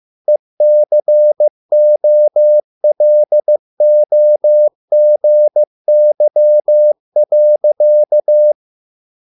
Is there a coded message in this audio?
Yes: Morse code from 0.3 to 8.5 s, heard as a keyed tone, about 15 wpm, 600 Hz. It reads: ECOLOGY.